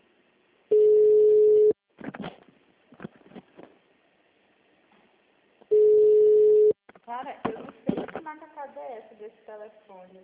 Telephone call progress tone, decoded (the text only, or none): ringback tone